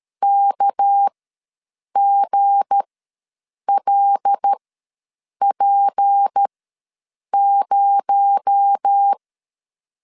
Morse code, heard as a keyed tone, about 13 wpm, spaced as for 7 wpm, 800 Hz. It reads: KGLP0
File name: LEZIONI / 7 GF9Z / 7 G ws 12wpm 3140